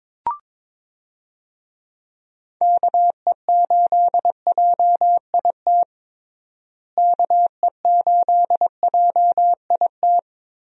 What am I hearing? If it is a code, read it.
Morse code, heard as a keyed tone, about 22 wpm, 700 Hz. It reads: KE8JIT KE8JIT
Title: USA Call Signs - 6 Characters - Repeated 2x - 22wpm